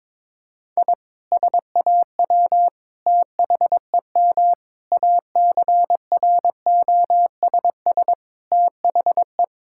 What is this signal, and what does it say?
Morse code, heard as a keyed tone, about 22 wpm, 700 Hz. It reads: I SAW THEM ACROSS THE
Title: Sentences from Top 1000 Words - Review Part 5 - 22wpm